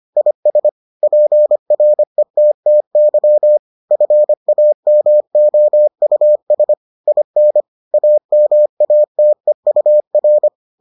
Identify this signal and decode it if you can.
Morse code, heard as a keyed tone, about 25 wpm, 600 Hz. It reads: IS PRETTY FAMOUS IN AMATEUR